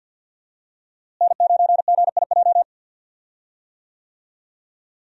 Morse code, heard as a keyed tone, about 49 wpm, 700 Hz. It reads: N9GIJ
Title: USA Call Signs - 5 Characters - Repeated 2x - 50wpm